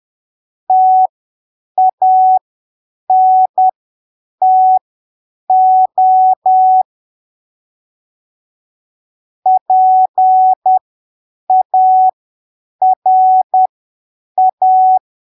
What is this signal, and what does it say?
Morse code, heard as a keyed tone, about 10 wpm, spaced as for 7 wpm, 750 Hz. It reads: TANTO PARA
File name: arquivos / texto / texto 21 10ppm